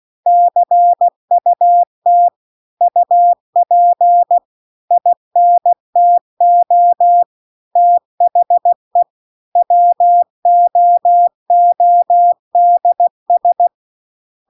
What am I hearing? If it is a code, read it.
Morse code, heard as a keyed tone, about 16 wpm, 700 Hz. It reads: CUT UP INTO THE WOODS